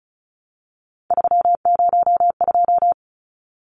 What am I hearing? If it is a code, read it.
Morse code, heard as a keyed tone, about 34 wpm, 700 Hz. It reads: 302
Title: Numbers - 3-Digit 35wpm